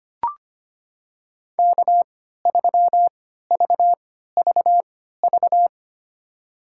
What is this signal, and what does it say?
Morse code, heard as a keyed tone, about 25 wpm, spaced as for 14 wpm, 700 Hz. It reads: K3VVV
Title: USA Call Signs - 5 Characters - 3x Character Spacing - 25wpm